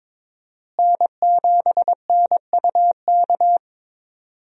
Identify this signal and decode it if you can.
Morse code, heard as a keyed tone, about 21 wpm, 700 Hz. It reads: N7NUK
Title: USA Call Signs - 5 Characters - Repeated 3x - 22wpm